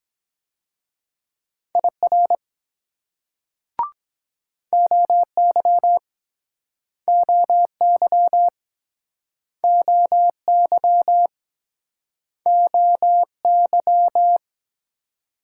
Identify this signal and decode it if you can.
Morse code, heard as a keyed tone, about 24 wpm, 700 Hz. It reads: IR OY OY OY OY